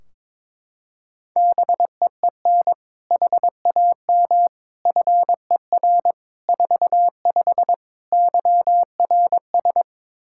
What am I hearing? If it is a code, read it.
Morse code, heard as a keyed tone, about 22 wpm, 700 Hz. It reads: BEEN HAM FER 45 YRS
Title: K1USN SST QSO Practice 22wpm